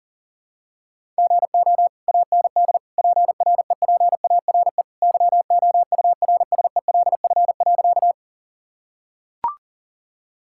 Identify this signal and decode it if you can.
Morse code, heard as a keyed tone, about 40 wpm, 700 Hz. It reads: GO AND PREPARE YOURSELF.
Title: Sentences from Top 1000 Words - Review Part 6 - 40wpm